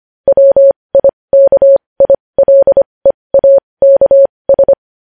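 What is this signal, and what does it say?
Morse code, heard as a keyed tone, about 25 wpm, spaced as for 20 wpm, 550 Hz. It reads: WIKILEAKS
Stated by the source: Morse code practice of news headlines at 20 wpm.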